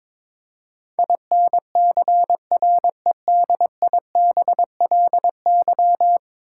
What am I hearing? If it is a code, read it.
Morse code, heard as a keyed tone, about 22 wpm, 700 Hz. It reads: INCREDIBLY
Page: morse-code-ninja-practice-22wpm